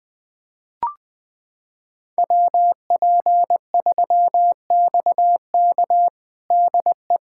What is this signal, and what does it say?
Morse code, heard as a keyed tone, about 20 wpm, 700 Hz. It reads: WP3XK DE
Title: USA Callsign DE Callsign - 5 Characters - 20wpm